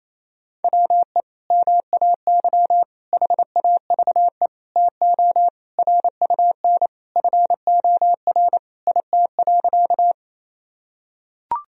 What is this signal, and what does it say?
Morse code, heard as a keyed tone, about 28 wpm, 700 Hz. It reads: WE MAY HAVE TO RUN FOR IT.
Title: Sentences from Top 1000 Words - Review Part 1 - 28wpm